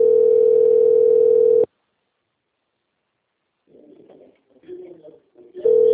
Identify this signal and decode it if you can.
Ringback tone (North America), 440 + 480 Hz.
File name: out- - - - - .15738